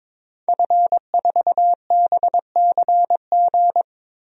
Morse code, heard as a keyed tone, about 22 wpm, 700 Hz. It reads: F4BCG
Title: DXCC-100 Call Signs - 5 Characters - 22wpm